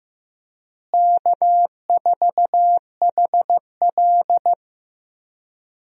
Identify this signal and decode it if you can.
Morse code, heard as a keyed tone, about 15 wpm, 700 Hz. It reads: K4HL
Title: USA Call Signs - 4 Characters - Repeated 3x - 15wpm